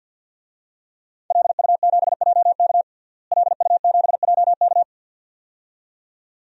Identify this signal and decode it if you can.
Morse code, heard as a keyed tone, about 52 wpm, 700 Hz. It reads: LU7JK LU7JK
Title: DXCC-100 Call Signs - 5 Characters - Repeated 2x - 50wpm